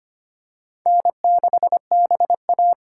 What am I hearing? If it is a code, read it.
Morse code, heard as a keyed tone, about 25 wpm, 700 Hz. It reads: N6BA
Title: USA Call Signs - 4x Character Spacing - 25wpm